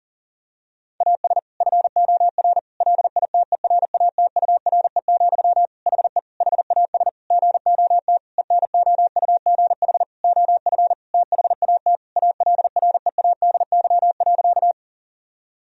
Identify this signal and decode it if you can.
Morse code, heard as a keyed tone, about 40 wpm, 700 Hz. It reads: AS FOR LITERATURE, HE HAS GOT ENOUGH OF THAT ALREADY.
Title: Sentences from Top 1000 Words - Review Part 3 - 40wpm